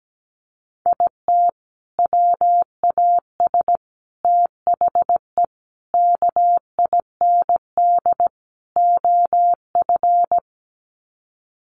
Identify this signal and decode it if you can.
Morse code, heard as a keyed tone, about 17 wpm, 700 Hz. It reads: IT WAS THE KIND OF
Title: Sets of 5 Words Encore2 17wpm